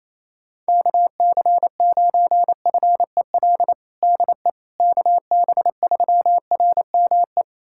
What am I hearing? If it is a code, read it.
Morse code, heard as a keyed tone, about 28 wpm, 700 Hz. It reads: KC9FEL DE KB3RME
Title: USA Callsign DE Callsign - 6 Characters - 28wpm